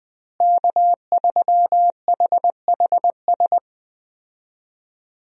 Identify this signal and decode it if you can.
Morse code, heard as a keyed tone, about 19 wpm, 700 Hz. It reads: K3HHS
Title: USA Call Signs - 5 Characters - Repeated 3x - 20wpm